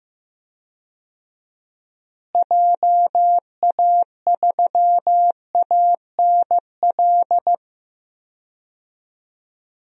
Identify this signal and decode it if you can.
Morse code, heard as a keyed tone, about 16 wpm, 700 Hz. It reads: JA3ANL
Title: DXCC-100 Call Signs - 6 Characters - Repeated 2x - 15wpm